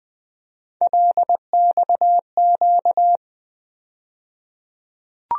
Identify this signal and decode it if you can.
Morse code, heard as a keyed tone, about 20 wpm, 700 Hz. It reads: LXQ
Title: USA Call Sign Postfixes - 3 Letters 20wpm